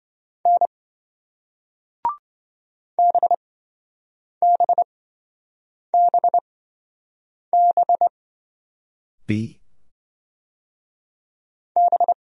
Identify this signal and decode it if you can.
Morse code, heard as a keyed tone, about 28 wpm, spaced as for 7 wpm, 700 Hz. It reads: N BBBB B